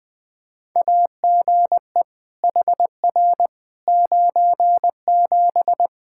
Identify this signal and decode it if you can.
Morse code, heard as a keyed tone, about 20 wpm, 700 Hz. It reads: AGE HR 97
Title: Practice QSO - Exchange 3 - More Info - 20wpm